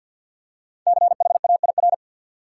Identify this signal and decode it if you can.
Morse code, heard as a keyed tone, about 53 wpm, 700 Hz. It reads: CHAIR
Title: Five Letter Words - 50wpm